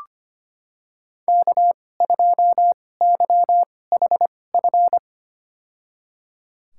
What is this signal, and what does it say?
Morse code, heard as a keyed tone, about 25 wpm, spaced as for 18 wpm, 700 Hz. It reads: K2YHF
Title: USA Call Signs - 5 Characters - 2x Character Spacing - 25wpm